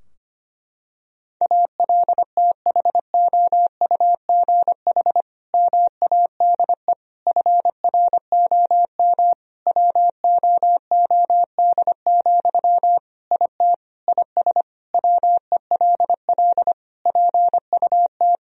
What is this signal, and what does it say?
Morse code, heard as a keyed tone, about 25 wpm, 700 Hz. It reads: ALTHOUGH MADE FROM WOOD, IT IS WELL PUT
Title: Sentences from Top 1000 Words - Review Part 3 - 25wpm